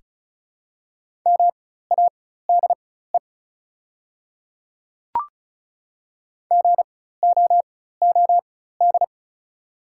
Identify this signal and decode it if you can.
Morse code, heard as a keyed tone, about 36 wpm, spaced as for 16 wpm, 700 Hz. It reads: MADE GOOD
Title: 4x Character Spacing - Top 200 Words - 35wpm